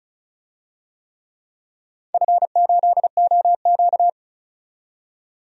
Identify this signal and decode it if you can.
Morse code, heard as a keyed tone, about 36 wpm, 690 Hz. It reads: F8OQ
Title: DXCC-100 Call Signs - 4 Characters - 35wpm